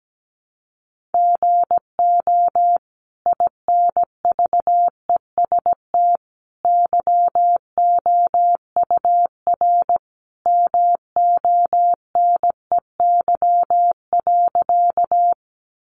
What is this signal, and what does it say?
Morse code, heard as a keyed tone, about 17 wpm, 700 Hz. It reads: GO INVEST YOUR MONEY.
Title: Sentences from Top 1100 Words 17wpm